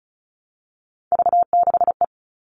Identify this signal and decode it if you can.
Morse code, heard as a keyed tone, about 34 wpm, 700 Hz. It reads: V6E
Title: All Cut Numbers - 3-Digit 35wpm